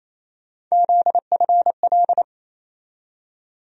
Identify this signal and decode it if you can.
Morse code, heard as a keyed tone, about 28 wpm, 700 Hz. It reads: ZFL
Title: USA Call Sign Postfixes - 3 Letters 28wpm